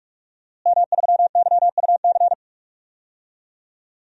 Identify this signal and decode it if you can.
Morse code, heard as a keyed tone, about 45 wpm, 700 Hz. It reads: M3YUC
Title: DXCC-100 Call Signs - 5 Characters - Repeated 3x - 45wpm